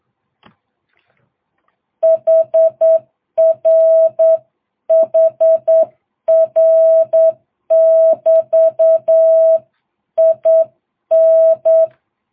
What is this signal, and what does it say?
Morse code, heard as a keyed tone, about 8 wpm, 660 Hz. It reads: HR HR <BT> IN